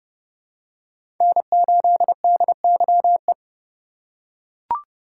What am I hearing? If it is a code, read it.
Morse code, heard as a keyed tone, about 30 wpm, 700 Hz. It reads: N8DYE
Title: USA Call Signs - 5 Characters - 2x Character Spacing - 30wpm